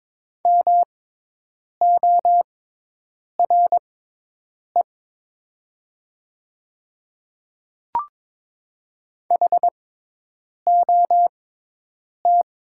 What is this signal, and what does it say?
Morse code, heard as a keyed tone, about 22 wpm, spaced as for 8 wpm, 700 Hz. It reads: MORE HOT
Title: 6x Character Spacing - Top 200 Words - 22wpm